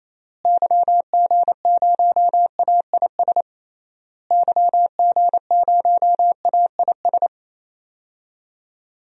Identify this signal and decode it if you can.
Morse code, heard as a keyed tone, about 28 wpm, 700 Hz. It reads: YG0AIS YG0AIS
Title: DXCC-100 Call Signs - 6 Characters - Repeated 3x - 28wpm